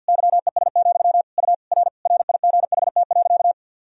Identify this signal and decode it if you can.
Morse code, heard as a keyed tone, about 52 wpm, 700 Hz. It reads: YES, U R RIGHT.
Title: Sentences from Top 500 Words and Abbreviations 50wpm